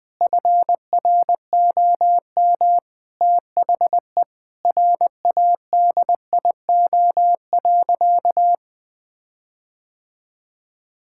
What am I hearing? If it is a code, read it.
Morse code, heard as a keyed tone, about 20 wpm, 700 Hz. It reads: FROM THE RADIO.
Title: Sentences with 5-Word Phrases - 20wpm